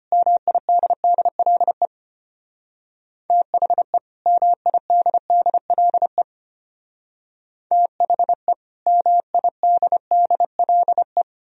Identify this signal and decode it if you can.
Morse code, heard as a keyed tone, about 31 wpm, 700 Hz. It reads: MIDDLE THE MIDDLE THE MIDDLE
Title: Speed-Racing - Sets of 2 Words 25wpm